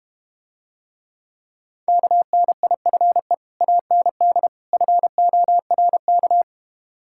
Morse code, heard as a keyed tone, about 31 wpm, 690 Hz. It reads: KNIFE AND FORK
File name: Easy Words - Speed-Racing with Hints - 25wpm